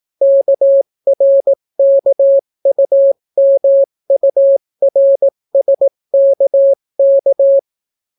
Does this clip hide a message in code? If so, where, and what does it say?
Yes: Morse code from 0.2 to 7.6 s, heard as a keyed tone, about 18 wpm, 550 Hz. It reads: KRKUMURSKK